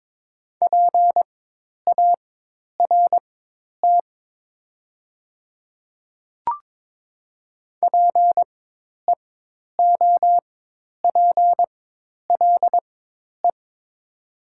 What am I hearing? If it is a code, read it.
Morse code, heard as a keyed tone, about 22 wpm, spaced as for 10 wpm, 700 Hz. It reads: PART PEOPLE
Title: 4x Character Spacing - Top 300 Words - 22wpm